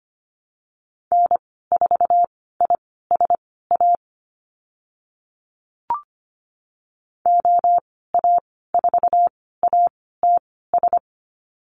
Morse code, heard as a keyed tone, about 25 wpm, spaced as for 16 wpm, 690 Hz. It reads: N4ISA OA4ATS
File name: Lesson 15g - Number 5 - Call Signs 25wpm